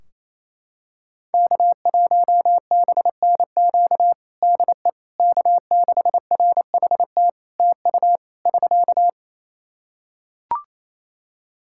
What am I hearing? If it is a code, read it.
Morse code, heard as a keyed tone, about 28 wpm, 700 Hz. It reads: K1BNQ DE K6RHT TU <SK>
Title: K1USN SST QSO Practice 28wpm